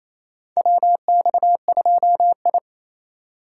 Morse code, heard as a keyed tone, about 28 wpm, 700 Hz. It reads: WX2I
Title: USA Call Signs - 4 Characters - 2x Character Spacing - 28wpm